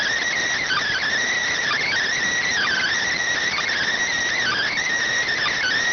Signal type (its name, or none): Emergency Alert System (SAME) tones